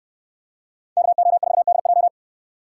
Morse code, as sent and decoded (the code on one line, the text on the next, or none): -.- --- ....- -.. .---
KO4DJ